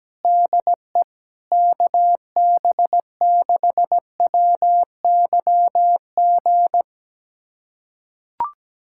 Morse code, heard as a keyed tone, about 17 wpm, 700 Hz. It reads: DE KB6WYG
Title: USA Callsign DE Callsign - 6 Characters - 17wpm